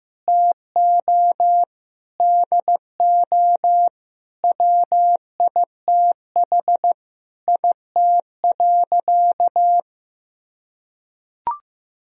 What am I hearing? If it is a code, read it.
Morse code, heard as a keyed tone, about 15 wpm, 700 Hz. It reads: TO DO WITH IT.
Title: Sentences from Top 1000 Words - Review Part 2 - 15wpm